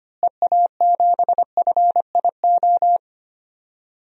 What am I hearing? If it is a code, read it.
Morse code, heard as a keyed tone, about 25 wpm, 700 Hz. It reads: EA7FIO
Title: DXCC-100 Call Signs - 6 Characters - Repeated 2x - 25wpm